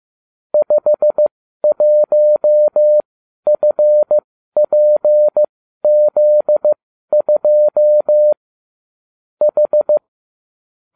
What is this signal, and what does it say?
Morse code, heard as a keyed tone, about 15 wpm, spaced as for 12 wpm, 600 Hz. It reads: 51FPZ2 H